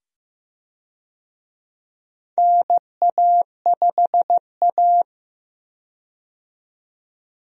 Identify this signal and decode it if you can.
Morse code, heard as a keyed tone, about 15 wpm, 700 Hz. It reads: NA5A
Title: USA Call Signs - 4 Characters - 15wpm